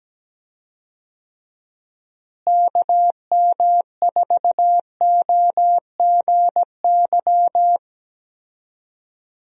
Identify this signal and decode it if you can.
Morse code, heard as a keyed tone, about 17 wpm, 700 Hz. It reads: KM4OGY
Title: USA Call Signs - 6 Characters - Repeated 3x - 17wpm